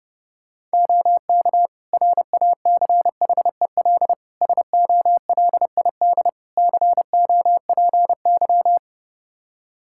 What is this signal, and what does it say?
Morse code, heard as a keyed tone, about 32 wpm, 700 Hz. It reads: OK RACHEL SOLID COPY
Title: K1USN SST QSO Practice 30wpm